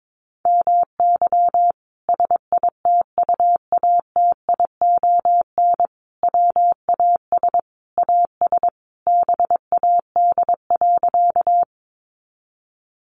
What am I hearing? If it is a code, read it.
Morse code, heard as a keyed tone, about 22 wpm, 700 Hz. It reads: MY SITUATION WAS AS BAD.
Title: Sentences from Top 1200 Words 22wpm